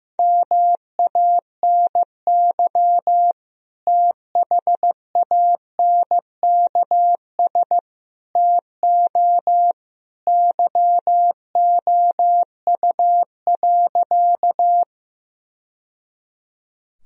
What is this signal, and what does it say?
Morse code, heard as a keyed tone, about 15 wpm, 700 Hz. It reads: MANY THANKS TO YOU.
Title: Sentences from Top 1000 Words - Review Part 3 - 15wpm